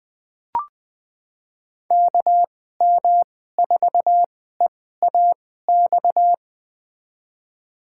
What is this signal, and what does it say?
Morse code, heard as a keyed tone, about 21 wpm, spaced as for 15 wpm, 700 Hz. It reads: KM4EAX
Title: USA Call Signs - 2x Character Spacing - 20wpm